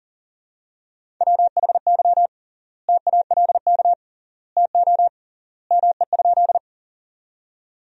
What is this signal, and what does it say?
Morse code, heard as a keyed tone, about 40 wpm, 700 Hz. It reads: WHY TALK TO ME?
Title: 3x Word-Spacing - Sentences from Top 400 Words - 40wpm